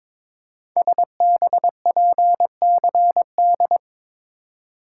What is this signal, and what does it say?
Morse code, heard as a keyed tone, about 22 wpm, 700 Hz. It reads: SBPCD
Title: Code Groups - 5 Letters - 22wpm